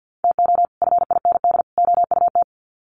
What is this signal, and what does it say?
Morse code, heard as a keyed tone, about 50 wpm, 700 Hz. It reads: TO FIND OUT
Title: Sets of 3 Words 50wpm